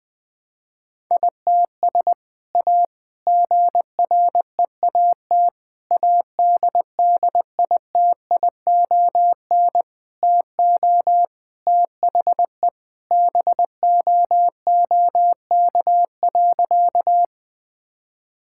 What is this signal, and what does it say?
Morse code, heard as a keyed tone, about 20 wpm, 700 Hz. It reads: ITS A GREAT ADDITION TO THE BOOK.
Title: Sentences from Top 1000 Words - Review Part 6 - 20wpm